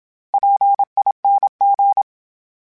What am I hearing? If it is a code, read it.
Morse code, heard as a keyed tone, about 27 wpm, 800 Hz. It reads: PING
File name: Special INGs-mp3s / Special INGs-mp3s / Sngl Ing @25